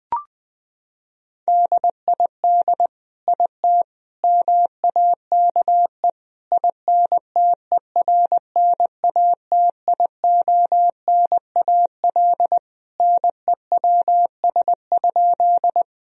Morse code, heard as a keyed tone, about 20 wpm, 700 Hz. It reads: DID IT MAKE INTERNATIONAL NEWS?
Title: Sentences from Top 1000 Words - Review Part 4 - 20wpm